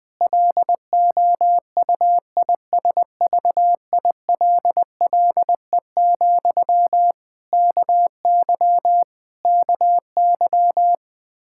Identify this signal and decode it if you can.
Morse code, heard as a keyed tone, about 20 wpm, 700 Hz. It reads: LOUISVILLE, KY KY
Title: Practice QSO - Exchange 1 - Beginning - 20wpm